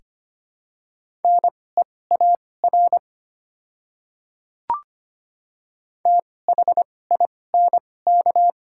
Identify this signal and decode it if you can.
Morse code, heard as a keyed tone, about 25 wpm, spaced as for 18 wpm, 700 Hz. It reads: NEAR THINK
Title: 2x Character Spacing - Top 500 Words - 25wpm